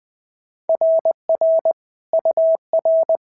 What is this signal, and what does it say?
Morse code, heard as a keyed tone, about 20 wpm, 650 Hz. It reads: RR UR
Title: SOTA QSO Practice 20wpm